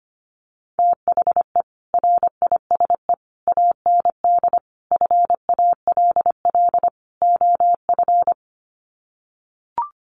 Morse code, heard as a keyed tone, about 25 wpm, 700 Hz. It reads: THE RISE AND FALL OF
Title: Sets of 5 Words Encore2 25wpm